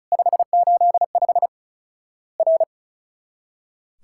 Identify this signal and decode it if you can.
Morse code, heard as a keyed tone, about 36 wpm, 700 Hz. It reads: 585 R
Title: Worked All Europe DX Contest - QTC Exchange - 35wpm